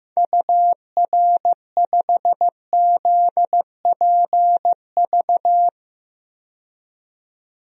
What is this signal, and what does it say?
Morse code, heard as a keyed tone, about 15 wpm, 700 Hz. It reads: UR5ZPV
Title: DXCC-100 Call Signs - 6 Characters - 15wpm